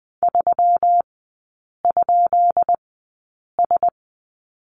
Morse code, heard as a keyed tone, about 20 wpm, spaced as for 8 wpm, 690 Hz. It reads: 3?S
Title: 133 Lesson 22e - Question Mark - Three Characters 20wpm